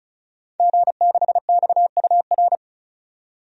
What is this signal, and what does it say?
Morse code, heard as a keyed tone, about 36 wpm, 700 Hz. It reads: G6XUR
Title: DXCC-100 Call Signs - 5 Characters - 35wpm